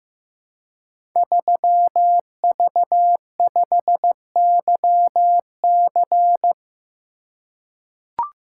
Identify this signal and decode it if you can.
Morse code, heard as a keyed tone, about 15 wpm, 700 Hz. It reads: 3V5YC
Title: Code Groups - 5 Letter:Numbers - 15wpm